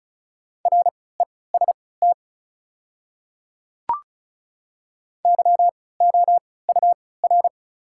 Morse code, heard as a keyed tone, about 36 wpm, spaced as for 20 wpm, 700 Hz. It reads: REST YOUR
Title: 3x Character Spacing - Top 500 Words - 35wpm